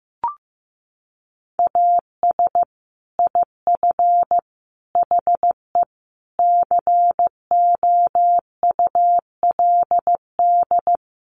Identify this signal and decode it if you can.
Morse code, heard as a keyed tone, about 15 wpm, 700 Hz. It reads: AS IF HE COULD
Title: Sentences from Top 1300 Words 15wpm